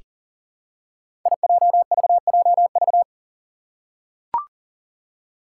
Morse code, heard as a keyed tone, about 42 wpm, 690 Hz. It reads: IJVJV
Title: Five Challenging Letter-Number Combinations 40wpm